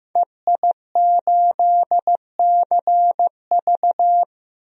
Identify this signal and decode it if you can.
Morse code, heard as a keyed tone, about 15 wpm, 700 Hz. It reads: EI8CV
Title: DXCC-100 Call Signs - 5 Characters - 15wpm